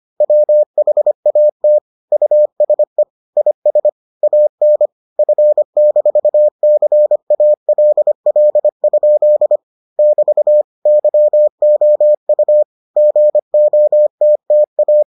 Morse code, heard as a keyed tone, about 25 wpm, 600 Hz. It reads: WHAT USE IS AN F-CALL? <BT> YOU GOTTA